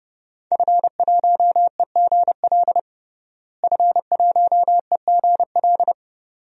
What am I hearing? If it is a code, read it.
Morse code, heard as a keyed tone, about 30 wpm, 700 Hz. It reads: F1EGL F1EGL
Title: DXCC-100 Call Signs - 5 Characters - Repeated 2x - 30wpm